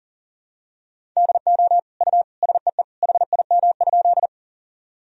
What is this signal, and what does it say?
Morse code, heard as a keyed tone, about 37 wpm, 700 Hz. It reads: DO U SEE HIM?
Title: Sentences from Top 100 Words and Abbreviations 40wpm